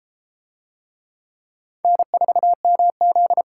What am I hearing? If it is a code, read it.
Morse code, heard as a keyed tone, about 30 wpm, 700 Hz. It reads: N4MZ